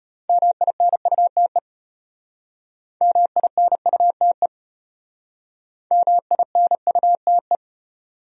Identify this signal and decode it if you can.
Morse code, heard as a keyed tone, about 32 wpm, 700 Hz. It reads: MINUTE MINUTE MINUTE